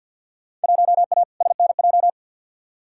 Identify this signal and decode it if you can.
Morse code, heard as a keyed tone, about 54 wpm, 700 Hz. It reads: 1A SNJ
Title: Field Day Exchanges - 50wpm